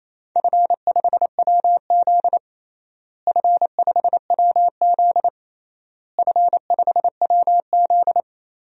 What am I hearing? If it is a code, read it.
Morse code, heard as a keyed tone, about 28 wpm, 700 Hz. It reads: F5WZ F5WZ F5WZ
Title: DXCC-100 Call Signs - 4 Characters - Repeated 3x - 28wpm